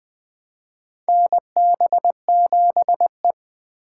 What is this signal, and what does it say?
Morse code, heard as a keyed tone, about 20 wpm, 700 Hz. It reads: NB7E
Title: USA Call Signs - 4 Characters - 4x Character Spacing - 20wpm